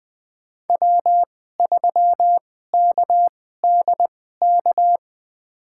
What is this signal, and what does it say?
Morse code, heard as a keyed tone, about 20 wpm, spaced as for 14 wpm, 700 Hz. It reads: W3KDK
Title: USA Call Signs - 5 Characters - 2x Character Spacing - 20wpm